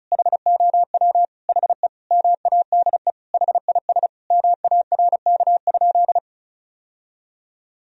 Morse code, heard as a keyed tone, about 34 wpm, 700 Hz. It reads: HOW HE MADE HIS MARK?
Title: Sentences from Top 1000 Words - Review Part 1 - 35wpm